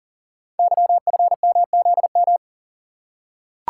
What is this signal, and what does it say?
Morse code, heard as a keyed tone, about 42 wpm, 700 Hz. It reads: YFMZM
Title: Code Groups - 5 Letters - 40wpm